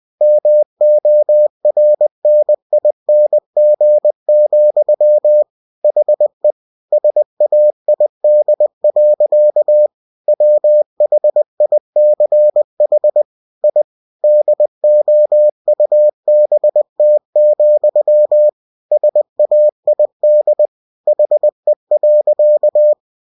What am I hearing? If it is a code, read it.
Morse code, heard as a keyed tone, about 20 wpm, 600 Hz. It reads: MORNING, HE SAID. WHICH I DOUBT, SAID HE.
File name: cw-books / winnie-the-pooh / winnie-20wpm-page-0053